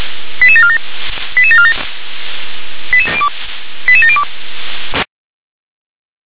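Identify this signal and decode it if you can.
Five-tone selective call (ZVEI), heard as quick runs of five tones, tones of about 70 ms. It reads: 88747 88747 88702
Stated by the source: ZVEI five-tone selective call received by SDR.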